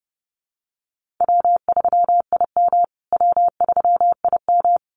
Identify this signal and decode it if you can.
Morse code, heard as a keyed tone, about 30 wpm, 700 Hz. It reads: W3IM W3IM
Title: USA Call Signs Repeated 2x 30wpm